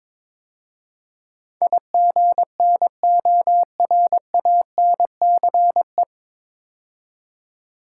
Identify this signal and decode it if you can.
Morse code, heard as a keyed tone, about 22 wpm, 700 Hz. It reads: IGNORANCE
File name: 9 Letter Words - First 3 Letters Given - 22wpm